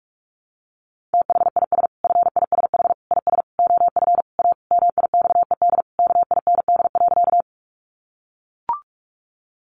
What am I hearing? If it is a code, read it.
Morse code, heard as a keyed tone, about 44 wpm, 700 Hz. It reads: THIS FISH IS OF A MIXED KIND.
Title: Sentences from Top 1500 Words 45wpm